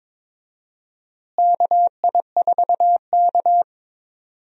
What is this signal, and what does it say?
Morse code, heard as a keyed tone, about 22 wpm, 700 Hz. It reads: KI4K
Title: USA Call Signs - 4 Characters - Repeated 3x - 22wpm